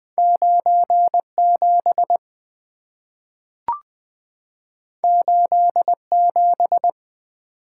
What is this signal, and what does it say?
Morse code, heard as a keyed tone, about 20 wpm, 700 Hz. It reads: 9 7 8 7
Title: Two Letter-Number Combinations - 20wpm